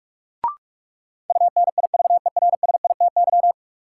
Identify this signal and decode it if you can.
Morse code, heard as a keyed tone, about 46 wpm, 700 Hz. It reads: UNIVERSITY
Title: Sets of 4 Words - New Word Drill - 45wpm